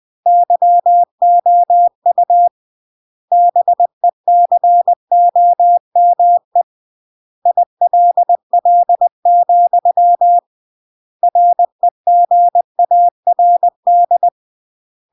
Morse code, heard as a keyed tone, about 20 wpm, 700 Hz. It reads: YOU BECOME ILL, REGARD
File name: CW / jodorowsky / 82 commandments 20-20